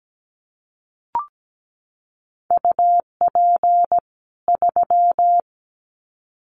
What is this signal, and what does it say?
Morse code, heard as a keyed tone, about 17 wpm, 700 Hz. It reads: UP 3
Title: Common QSO Phrases 17wpm